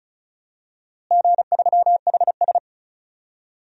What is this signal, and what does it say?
Morse code, heard as a keyed tone, about 36 wpm, 700 Hz. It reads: G3HS
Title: DXCC-100 Call Signs - 4 Characters - Repeated 3x - 35wpm